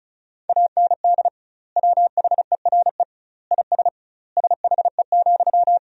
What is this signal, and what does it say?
Morse code, heard as a keyed tone, about 34 wpm, 700 Hz. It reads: AND WHERE IS SHE,
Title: 2x Word-Spacing - Sentences from Top 200 Words - 35wpm